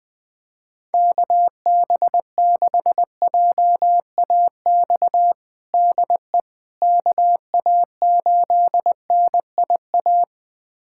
Morse code, heard as a keyed tone, about 21 wpm, 700 Hz. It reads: KB6JAX DE KA8NIA
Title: USA Callsign DE Callsign - 6 Characters - 20wpm